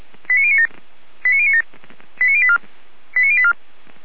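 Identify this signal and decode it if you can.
Five-tone selective call (ZVEI), heard as quick runs of five tones, tones of about 70 ms. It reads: 89097 89097 89084 89084
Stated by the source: Five-tone ZVEI call 89080, received by SDR.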